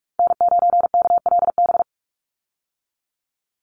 Morse code, heard as a keyed tone, about 44 wpm, 700 Hz. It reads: N9KLB
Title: USA Call Signs 45wpm